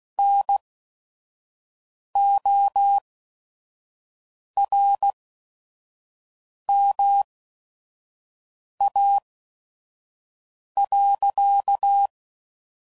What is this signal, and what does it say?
Morse code, heard as a keyed tone, about 16 wpm, spaced as for 5 wpm, 800 Hz. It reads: NORMA.
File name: CW Code Groups @5WPM / QSOs / WD0FBT 2